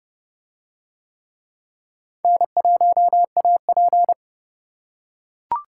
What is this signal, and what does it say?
Morse code, heard as a keyed tone, about 28 wpm, 700 Hz. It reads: N1AP